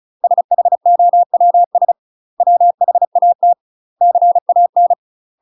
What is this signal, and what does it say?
Morse code, heard as a keyed tone, about 34 wpm, 700 Hz. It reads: SHOWS WHAT CAN
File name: UK MASTER QSOs TRAINING / 35WPM / UK MASTER QSOs 0000